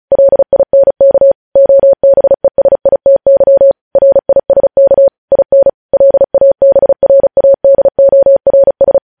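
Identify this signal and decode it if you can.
Morse code, heard as a keyed tone, about 36 wpm, 550 Hz. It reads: LINK OBESITY RISK IN LABRADORS
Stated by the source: News headlines sent in Morse code at 35 WPM.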